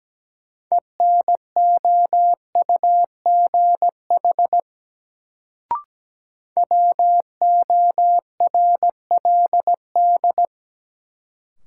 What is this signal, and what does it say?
Morse code, heard as a keyed tone, about 17 wpm, 700 Hz. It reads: ENOUGH WORLD
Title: Sets of 2 Words - New Word Drill - 17wpm